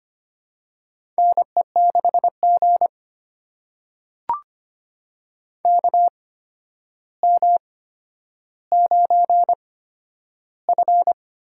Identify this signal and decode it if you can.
Morse code, heard as a keyed tone, about 25 wpm, spaced as for 11 wpm, 700 Hz. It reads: NE6G KM9F
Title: USA Call Signs - 4 Characters - 8x Character Spacing - 25wpm